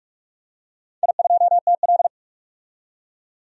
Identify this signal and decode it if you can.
Morse code, heard as a keyed tone, about 46 wpm, 700 Hz. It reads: I2TL
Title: DXCC-100 Call Signs - 4 Characters - 45wpm